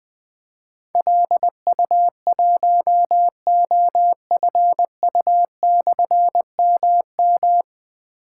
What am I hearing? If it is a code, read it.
Morse code, heard as a keyed tone, about 20 wpm, 700 Hz. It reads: LU1OFU/MM
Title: Difficult Call Signs 20wpm